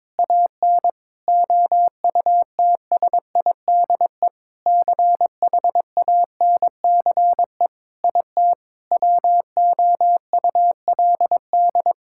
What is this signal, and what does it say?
Morse code, heard as a keyed tone, about 22 wpm, 700 Hz. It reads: AN OUTSIDE CHANCE IT WOULD
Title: Sentences from Top 1000 Words - Review Part 6 - 22wpm